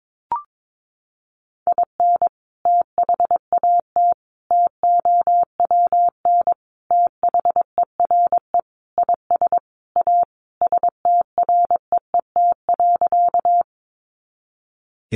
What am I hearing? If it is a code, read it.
Morse code, heard as a keyed tone, about 22 wpm, 700 Hz. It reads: IN THAT TOWN THERE IS A STREET.
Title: Sentences from Top 500 Words 22wpm